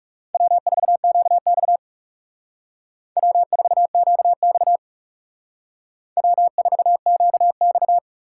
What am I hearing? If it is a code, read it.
Morse code, heard as a keyed tone, about 34 wpm, 700 Hz. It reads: W4QX W4QX W4QX
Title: Speed Racing - USA 1x2 Call Signs 30wpm